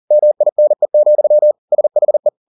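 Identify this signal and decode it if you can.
Morse code, heard as a keyed tone, about 40 wpm, 600 Hz. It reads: MINE, SHE
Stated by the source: Morse code at 40 wpm, practice text, CW med Gnister.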